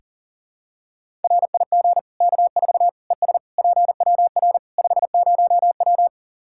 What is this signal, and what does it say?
Morse code, heard as a keyed tone, about 40 wpm, 700 Hz. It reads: RIG K4 ES PWR 50W
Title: K1USN SST QSO Practice 40wpm